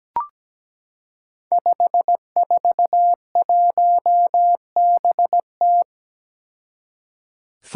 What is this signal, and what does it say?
Morse code, heard as a keyed tone, about 17 wpm, 700 Hz. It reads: 541BT
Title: Five Challenging Letter-Number Combinations 17wpm